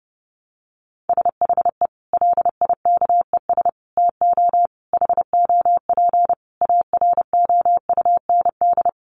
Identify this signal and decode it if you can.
Morse code, heard as a keyed tone, about 30 wpm, 700 Hz. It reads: SHE LIKES TO HOP AROUND
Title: Easy Phrases 30wpm